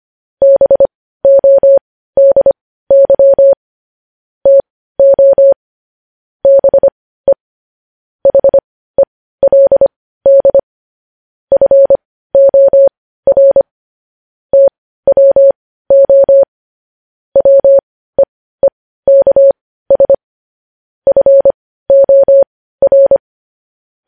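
Morse code, heard as a keyed tone, about 25 wpm, spaced as for 15 wpm, 550 Hz. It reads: BODY TO BE HELD FOR TWO WEEKS FOR
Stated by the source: Morse code news headlines practice at 15, preceded by vvv.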